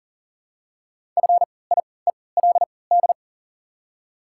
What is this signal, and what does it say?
Morse code, heard as a keyed tone, about 40 wpm, spaced as for 23 wpm, 700 Hz. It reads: FIELD
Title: 3x Character Spacing - Top 500 Words - 40wpm